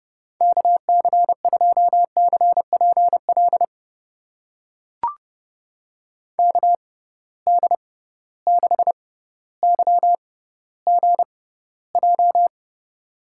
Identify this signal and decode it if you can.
Morse code, heard as a keyed tone, about 30 wpm, spaced as for 15 wpm, 700 Hz. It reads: KC2CPL KD6YGJ
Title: USA Call Signs - 6 Characters - 6x Character Spacing - 30wpm